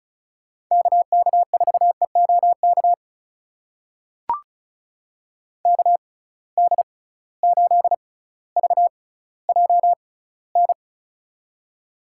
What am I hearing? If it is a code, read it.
Morse code, heard as a keyed tone, about 36 wpm, spaced as for 18 wpm, 690 Hz. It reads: KK4EOK KD8VJN
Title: USA Call Signs - 6 Characters - 6x Character Spacing - 35wpm